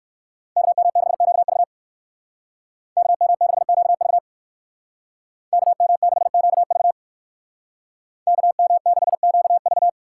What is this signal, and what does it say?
Morse code, heard as a keyed tone, about 60 wpm, 700 Hz. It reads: KM6QV KM6QV KM6QV KM6QV